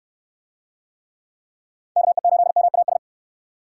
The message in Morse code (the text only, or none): KE8MAS